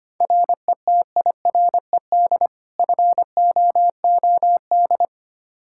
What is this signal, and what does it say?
Morse code, heard as a keyed tone, about 25 wpm, 700 Hz. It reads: RETIRED FOOD
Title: Practice QSO - Exchange 3 - More Info - 25wpm